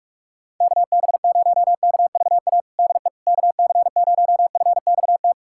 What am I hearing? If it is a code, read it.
Morse code, heard as a keyed tone, about 48 wpm, 700 Hz. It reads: KB0KVA DE KC0FXT
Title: USA Callsign DE Callsign - 6 Characters - 45wpm